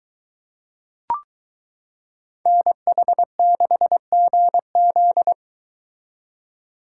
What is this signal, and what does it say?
Morse code, heard as a keyed tone, about 23 wpm, 700 Hz. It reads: NH6GZ